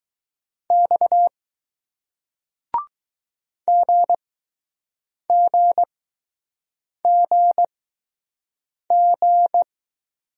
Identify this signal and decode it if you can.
Morse code, heard as a keyed tone, about 20 wpm, spaced as for 6 wpm, 700 Hz. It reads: X GGGG